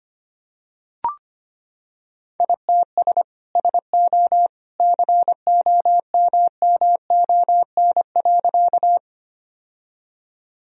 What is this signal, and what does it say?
Morse code, heard as a keyed tone, about 25 wpm, 700 Hz. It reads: ITS SO COMMON.
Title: Sentences from Top 1000 Words - Review Part 3 - 25wpm